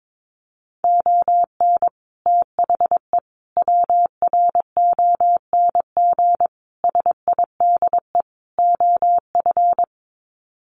Morse code, heard as a keyed tone, about 22 wpm, 700 Hz. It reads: ON THE WRONG SIDE OF
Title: Sets of 5 Words Encore 22wpm